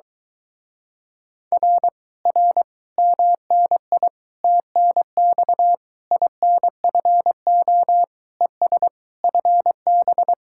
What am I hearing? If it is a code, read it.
Morse code, heard as a keyed tone, about 23 wpm, 700 Hz. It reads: R R MNI TNX INFO ES FB